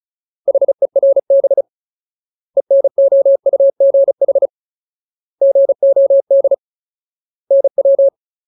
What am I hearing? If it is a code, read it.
Morse code, heard as a keyed tone, about 34 wpm, 550 Hz. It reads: HERB ENOUGH GOD NW
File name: Ultimate Word List / 35WPM Ultimate